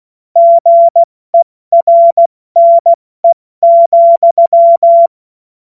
Morse code, heard as a keyed tone, about 16 wpm, 680 Hz. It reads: GERNE,
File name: Morsekurs / Witze / Witz 06 Februar 2024 70 F80